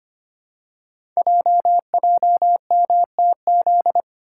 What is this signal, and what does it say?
Morse code, heard as a keyed tone, about 25 wpm, 700 Hz. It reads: JJMTZ
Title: Code Groups - 5 Letters - 25wpm